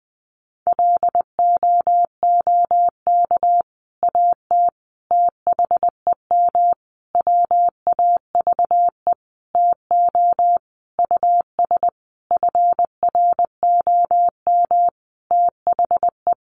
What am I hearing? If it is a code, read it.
Morse code, heard as a keyed tone, about 19 wpm, 700 Hz. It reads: LOOK AT THEM WAVE TO US FROM THE
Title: Sentences from Top 1400 Words 20wpm